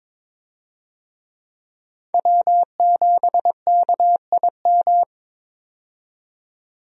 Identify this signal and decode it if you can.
Morse code, heard as a keyed tone, about 22 wpm, 700 Hz. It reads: W7KIM
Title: USA Call Signs - 5 Characters - 22wpm